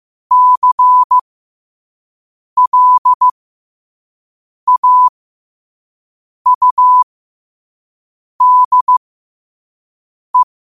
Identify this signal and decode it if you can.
Morse code, heard as a keyed tone, about 15 wpm, spaced as for 5 wpm, 1000 Hz. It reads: CLAUDE
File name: qotdfiles / 5wpm